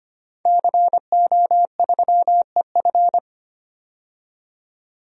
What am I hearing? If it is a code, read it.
Morse code, heard as a keyed tone, about 25 wpm, 700 Hz. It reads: CO3EF
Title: DXCC-100 Call Signs - 5 Characters - 25wpm